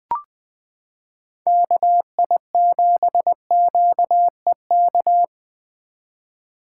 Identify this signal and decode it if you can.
Morse code, heard as a keyed tone, about 21 wpm, 700 Hz. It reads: KI7QEK
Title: 2023 SOTA Chaser Call Signs - 20wpm